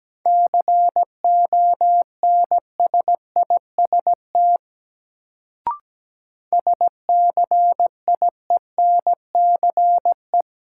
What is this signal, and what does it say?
Morse code, heard as a keyed tone, about 17 wpm, 700 Hz. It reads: CONSIST SCIENCE
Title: Sentences from Top 1400 Words - New Word Drill Alt - 17wpm